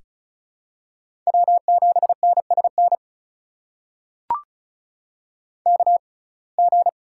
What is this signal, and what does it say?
Morse code, heard as a keyed tone, about 36 wpm, spaced as for 25 wpm, 700 Hz. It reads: W7NSN KG
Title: USA Call Signs - 6x Character Spacing - 35wpm